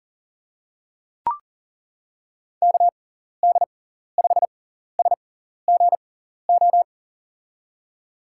Morse code, heard as a keyed tone, about 40 wpm, spaced as for 14 wpm, 700 Hz. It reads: KD5SGO
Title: USA Call Signs - 6 Characters - 6x Character Spacing - 40wpm